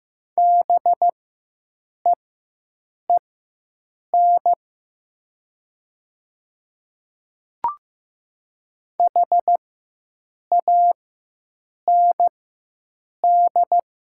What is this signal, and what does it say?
Morse code, heard as a keyed tone, about 15 wpm, spaced as for 7 wpm, 700 Hz. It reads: BEEN HAND
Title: 4x Character Spacing - Top 200 Words - 15wpm